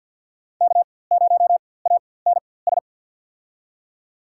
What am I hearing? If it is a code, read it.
Morse code, heard as a keyed tone, about 49 wpm, spaced as for 23 wpm, 700 Hz. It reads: K0ANS
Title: USA Call Signs - 5 Characters - 4x Character Spacing - 50wpm